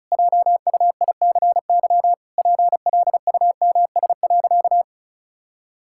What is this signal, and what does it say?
Morse code, heard as a keyed tone, about 34 wpm, 700 Hz. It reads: JUICY PLUMS.
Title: Pangram Challenge v2 - 35wpm